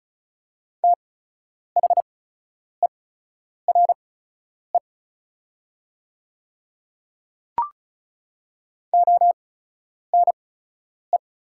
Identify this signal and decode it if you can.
Morse code, heard as a keyed tone, about 36 wpm, spaced as for 10 wpm, 700 Hz. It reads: THERE ONE
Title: 8x Character Spacing - Top 200 Words - 35wpm